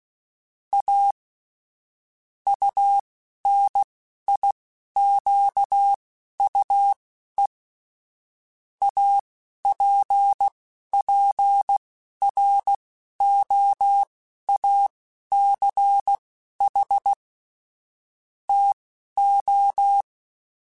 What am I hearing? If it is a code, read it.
Morse code, heard as a keyed tone, about 16 wpm, spaced as for 11 wpm, 800 Hz. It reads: A UNIQUE APPROACH TO